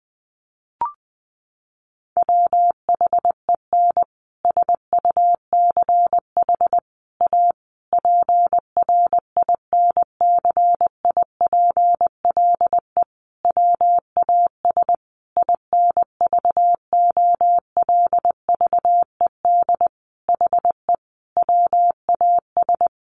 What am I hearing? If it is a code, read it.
Morse code, heard as a keyed tone, about 19 wpm, 700 Hz. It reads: WHEN SUCH A PRINCIPLE WAS INVOLVED HE WAS
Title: Sentences from Top 1400 Words 20wpm